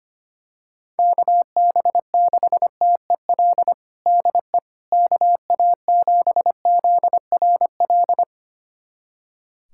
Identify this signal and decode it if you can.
Morse code, heard as a keyed tone, about 25 wpm, 700 Hz. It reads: KB6TEL DE KA7ZRL
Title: USA Callsign DE Callsign 25wpm